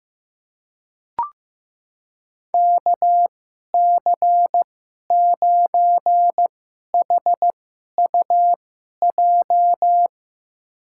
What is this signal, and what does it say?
Morse code, heard as a keyed tone, about 16 wpm, spaced as for 11 wpm, 700 Hz. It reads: KC9HUJ
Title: USA Call Signs - 2x Character Spacing - 15wpm